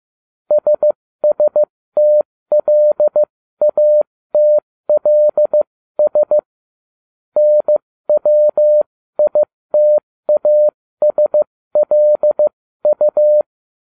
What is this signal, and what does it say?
Morse code, heard as a keyed tone, about 15 wpm, spaced as for 13 wpm, 600 Hz. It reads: SSTLATLS NWITASLU